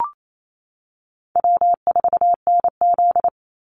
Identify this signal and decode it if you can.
Morse code, heard as a keyed tone, about 28 wpm, 700 Hz. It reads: W4NZ
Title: Easy Call Signs - 28wpm